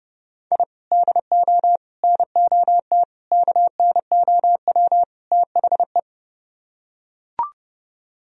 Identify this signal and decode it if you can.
Morse code, heard as a keyed tone, about 29 wpm, 700 Hz. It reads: I DO NOT KNOW THE